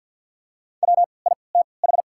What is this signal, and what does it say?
Morse code, heard as a keyed tone, about 54 wpm, spaced as for 29 wpm, 690 Hz. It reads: WITH
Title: 3x Character Spacing - Top 200 Words - 50wpm